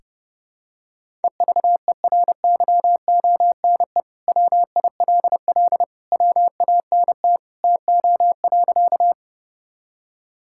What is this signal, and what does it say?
Morse code, heard as a keyed tone, about 32 wpm, 700 Hz. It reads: EVERYONE WILL WANT TO.
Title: Sentences from Top 1000 Words - Review Part 5 - 30wpm